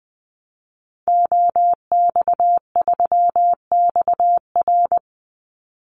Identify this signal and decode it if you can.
Morse code, heard as a keyed tone, about 20 wpm, 700 Hz. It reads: OX3XR
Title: Call Signs Repeated 3x - 20wpm